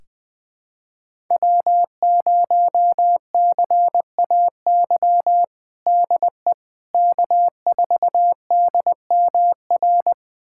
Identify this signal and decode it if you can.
Morse code, heard as a keyed tone, about 20 wpm, 700 Hz. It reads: W0CAY DE K4DMR
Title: USA Callsign DE Callsign - 5 Characters - 20wpm